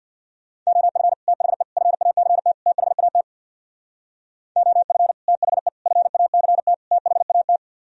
Morse code, heard as a keyed tone, about 62 wpm, 690 Hz. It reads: OF THE FACT THAT OF THE FACT THAT
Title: Speed-Racing - Sets of 4 Words 50wpm